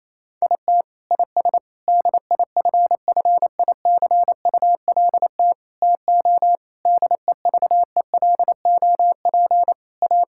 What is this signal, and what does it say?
Morse code, heard as a keyed tone, about 28 wpm, 700 Hz. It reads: IT IS DIFFICULT TO DEVELOP A
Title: Sentences from Top 1000 Words - Review Part 5 - 28wpm